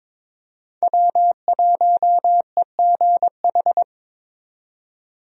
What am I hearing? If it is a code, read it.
Morse code, heard as a keyed tone, about 22 wpm, 700 Hz. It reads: W1EGH
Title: USA Call Signs - 5 Characters - 22wpm